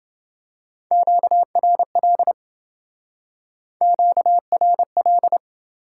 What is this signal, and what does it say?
Morse code, heard as a keyed tone, about 27 wpm, 700 Hz. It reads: QRL QRL